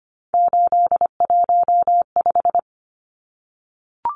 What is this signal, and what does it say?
Morse code, heard as a keyed tone, about 25 wpm, 700 Hz. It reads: 815
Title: Numbers - 3-Digit 25wpm